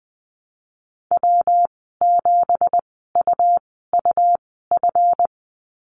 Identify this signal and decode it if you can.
Morse code, heard as a keyed tone, about 20 wpm, spaced as for 15 wpm, 700 Hz. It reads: W7UUF
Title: 173 Lesson 28g - Number 9 - Call Signs 20wpm